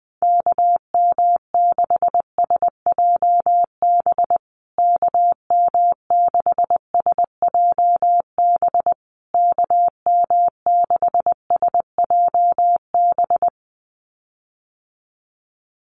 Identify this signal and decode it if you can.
Morse code, heard as a keyed tone, about 20 wpm, 700 Hz. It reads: KM6SJB KM6SJB KM6SJB
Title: USA Call Signs Repeated 3x 20wpm